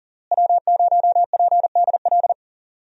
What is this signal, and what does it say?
Morse code, heard as a keyed tone, about 40 wpm, 700 Hz. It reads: W0PDL
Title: USA Call Signs - 5 Characters - Repeated 3x - 40wpm